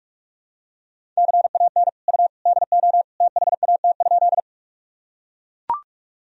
Morse code, heard as a keyed tone, about 41 wpm, 700 Hz. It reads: CAN U DO THAT?
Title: Sentences from Top 100 Words and Abbreviations 45wpm